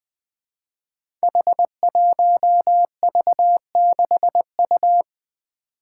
Morse code, heard as a keyed tone, about 20 wpm, 700 Hz. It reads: H1V6U
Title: Five Challenging Letter-Number Combinations 20wpm